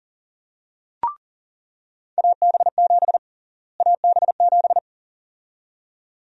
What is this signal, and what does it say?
Morse code, heard as a keyed tone, about 40 wpm, 700 Hz. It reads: AB7 AB7
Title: USA Call Sign Prefixes - Repeated 2x - 40wpm